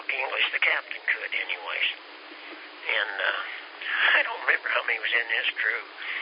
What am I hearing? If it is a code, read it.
Emergency Alert System (SAME) tones (header bursts), heard as harsh buzzing bursts, with speech. It is heard from 1.0 to 5.2 s.